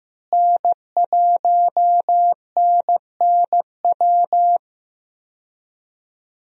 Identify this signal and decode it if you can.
Morse code, heard as a keyed tone, about 15 wpm, 700 Hz. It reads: N1NNW
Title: USA Call Signs - 5 Characters - Repeated 3x - 15wpm